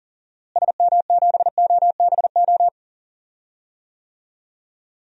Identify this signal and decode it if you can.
Morse code, heard as a keyed tone, about 37 wpm, 700 Hz. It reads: SM7OBO
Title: DXCC-100 Call Signs - 6 Characters - Repeated 3x - 40wpm